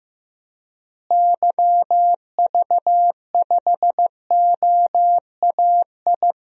Morse code, heard as a keyed tone, about 15 wpm, 700 Hz. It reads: YV5OAI
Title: DXCC-100 Call Signs - 6 Characters - Repeated 3x - 15wpm